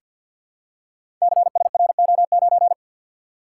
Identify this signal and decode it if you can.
Morse code, heard as a keyed tone, about 45 wpm, 700 Hz. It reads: CSRO9
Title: Code Groups - 5 Letter:Numbers - 50wpm